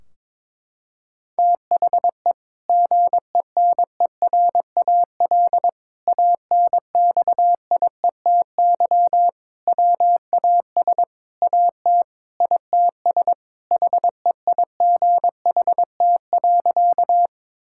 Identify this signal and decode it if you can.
Morse code, heard as a keyed tone, about 22 wpm, 700 Hz. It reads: THE GENERAL ANXIETY WAS AT ITS HEIGHT.
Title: Sentences from Top 1600 Words 22wpm